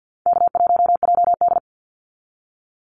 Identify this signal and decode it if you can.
Morse code, heard as a keyed tone, about 48 wpm, 700 Hz. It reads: K1JD
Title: NA SOTA Activator Callsigns 50wpm